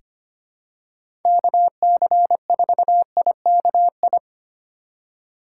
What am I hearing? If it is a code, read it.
Morse code, heard as a keyed tone, about 25 wpm, 700 Hz. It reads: KC4IKI
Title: USA Call Signs - 6 Characters - 3x Character Spacing - 25wpm